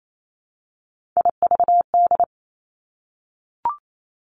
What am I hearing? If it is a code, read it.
Morse code, heard as a keyed tone, about 28 wpm, 690 Hz. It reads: IVD
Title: Three Challenging Letter-Number Combinations 28wpm